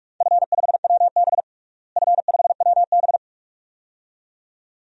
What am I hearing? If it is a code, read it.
Morse code, heard as a keyed tone, about 41 wpm, 700 Hz. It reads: F5WB F5WB
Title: DXCC-100 Call Signs - 4 Characters - Repeated 2x - 45wpm